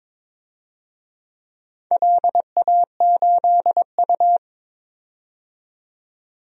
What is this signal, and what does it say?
Morse code, heard as a keyed tone, about 21 wpm, 700 Hz. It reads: LA8U
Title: DXCC-100 Call Signs - 4 Characters - 22wpm